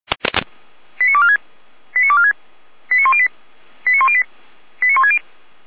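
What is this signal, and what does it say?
Five-tone selective call (ZVEI), heard as quick runs of five tones, tones of about 70 ms. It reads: 89246 89246 89108 89108 89160
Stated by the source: ZVEI five-tone selective call received on an SDR.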